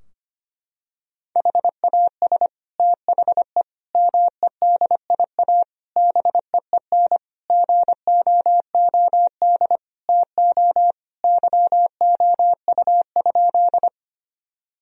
Morse code, heard as a keyed tone, about 25 wpm, 700 Hz. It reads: HAS THE MEDIA BEEN GOOD TO YOU?
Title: Sentences from Top 1000 Words - Review Part 4 - 25wpm